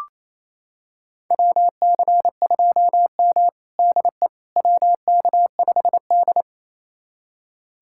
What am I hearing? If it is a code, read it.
Morse code, heard as a keyed tone, about 28 wpm, 700 Hz. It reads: WC2M DE WK5D
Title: USA Callsign DE Callsign - 4 Characters - 28wpm